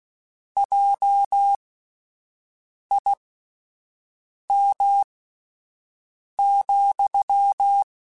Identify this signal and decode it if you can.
Morse code, heard as a keyed tone, about 16 wpm, spaced as for 5 wpm, 800 Hz. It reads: JIM,
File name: (spaced) / QSOs Spaced-mp3s / WD0BGP 1 @10 sp